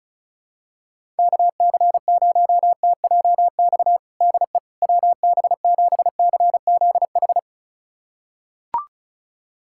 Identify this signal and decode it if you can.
Morse code, heard as a keyed tone, about 34 wpm, 700 Hz. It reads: KC0TJX DE WB7CZH
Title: USA Callsign DE Callsign 35wpm